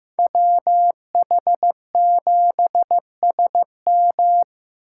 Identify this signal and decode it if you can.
Morse code, heard as a keyed tone, about 18 wpm, 700 Hz. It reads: WH7SM
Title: Speed Racing - USA 2x2 Call Signs 15wpm